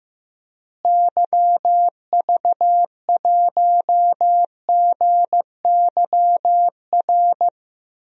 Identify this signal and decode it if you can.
Morse code, heard as a keyed tone, about 16 wpm, 700 Hz. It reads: YV1GYR
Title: DXCC-100 Call Signs - 6 Characters - Repeated 2x - 15wpm